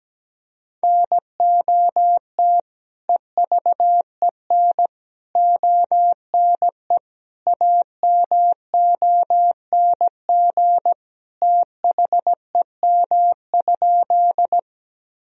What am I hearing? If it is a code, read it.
Morse code, heard as a keyed tone, about 17 wpm, 700 Hz. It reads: NOT EVEN ONE AMONG THEM?
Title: Sentences from Top 1000 Words - Review Part 3 - 17wpm